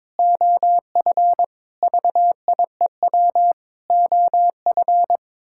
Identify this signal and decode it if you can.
Morse code, heard as a keyed tone, about 21 wpm, 700 Hz. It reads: OF VIEW OF
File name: practice / 22wpm / Sets of 6 Words - 22wpm